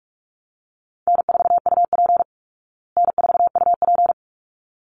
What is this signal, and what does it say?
Morse code, heard as a keyed tone, about 44 wpm, 700 Hz. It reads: N4UP N4UP
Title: Call Signs Repeated 3x - 45wpm